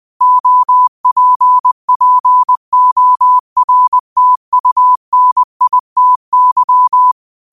Morse code, heard as a keyed tone, about 20 wpm, 1000 Hz. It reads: OPPORTUNITY
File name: qotdfiles / 20wpm